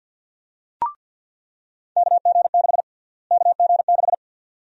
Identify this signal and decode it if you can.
Morse code, heard as a keyed tone, about 48 wpm, 700 Hz. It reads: KG6 KG6
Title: USA Call Sign Prefixes - Repeated 2x - 50wpm